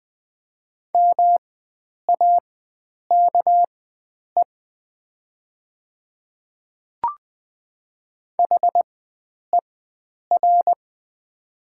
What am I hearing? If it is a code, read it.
Morse code, heard as a keyed tone, about 20 wpm, spaced as for 9 wpm, 690 Hz. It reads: MAKE HER
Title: 4x Character Spacing - Top 100 Words - 20wpm